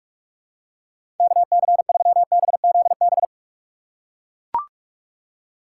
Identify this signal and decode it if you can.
Morse code, heard as a keyed tone, about 42 wpm, 700 Hz. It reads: KC3BZB
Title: USA Call Signs - 6 Characters - 4x Character Spacing - 45wpm